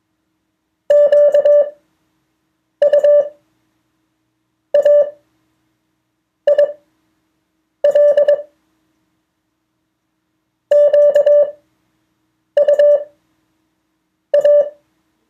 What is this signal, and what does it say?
Morse code, heard as a keyed tone, about 22 wpm, spaced as for 6 wpm, 580 Hz. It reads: QUAIL QUA
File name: 2of4-words-1